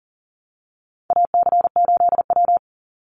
Morse code, heard as a keyed tone, about 40 wpm, 700 Hz. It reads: AC8W
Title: Call Signs Repeated 2x - 40wpm